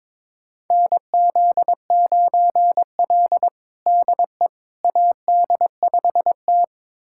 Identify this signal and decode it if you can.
Morse code, heard as a keyed tone, about 22 wpm, 700 Hz. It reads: NZ9L DE AD5T
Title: USA Callsign DE Callsign - 4 Characters - 22wpm